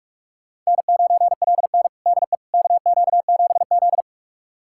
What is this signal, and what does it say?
Morse code, heard as a keyed tone, about 46 wpm, 700 Hz. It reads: N9LN DE KQ7Z
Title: USA Callsign DE Callsign - 4 Characters - 45wpm